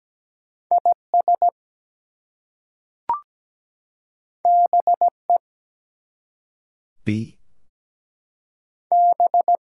Morse code, heard as a keyed tone, about 17 wpm, 700 Hz. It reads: IS BE B
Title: Two Letter Words - 17wpm